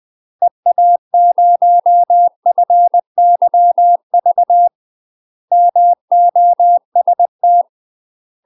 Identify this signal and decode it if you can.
Morse code, heard as a keyed tone, about 20 wpm, 700 Hz. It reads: EA0FYV MOST
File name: UK MASTER QSOs TRAINING / 20WPM / UK MASTER QSOs 0065